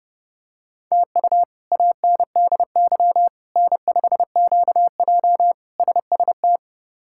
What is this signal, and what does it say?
Morse code, heard as a keyed tone, about 30 wpm, 700 Hz. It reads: TU ANDY N5QJ SST
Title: K1USN SST QSO Practice - 30wpm